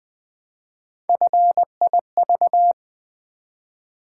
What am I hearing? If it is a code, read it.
Morse code, heard as a keyed tone, about 20 wpm, 700 Hz. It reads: FIV
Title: Build A Word - 1 Letter At A Time - 3 to 5-Letter Words - 20wpm